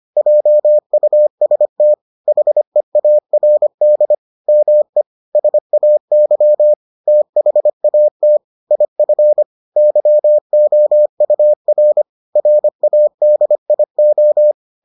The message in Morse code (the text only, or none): JUST HEARD ME SAY THAT IF YOUR RADIO